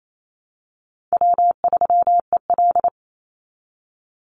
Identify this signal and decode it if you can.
Morse code, heard as a keyed tone, about 28 wpm, 700 Hz. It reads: W3EL
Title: Call Signs Repeated 2x - 28wpm